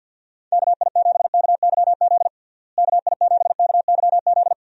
Morse code, heard as a keyed tone, about 46 wpm, 700 Hz. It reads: KI7KYZ KI7KYZ
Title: USA Call Signs - 6 Characters - Repeated 2x - 50wpm